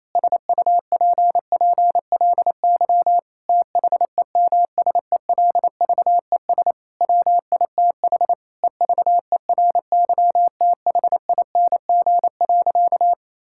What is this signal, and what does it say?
Morse code, heard as a keyed tone, about 28 wpm, 700 Hz. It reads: SUPPLY THEMSELVES WITH EVERYTHING.
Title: Sentences from Top 1000 Words - Review Part 5 - 28wpm